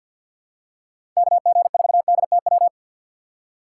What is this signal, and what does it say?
Morse code, heard as a keyed tone, about 48 wpm, 700 Hz. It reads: KG4DTW
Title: USA Call Signs - 6 Characters - 6x Character Spacing - 50wpm